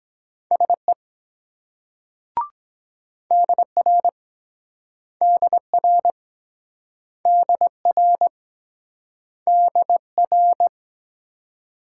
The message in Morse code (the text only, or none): SE DR DR DR DR